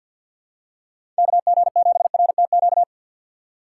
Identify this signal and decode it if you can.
Morse code, heard as a keyed tone, about 52 wpm, 700 Hz. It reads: KG7RTQ
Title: USA Call Signs - 6 Characters - 4x Character Spacing - 50wpm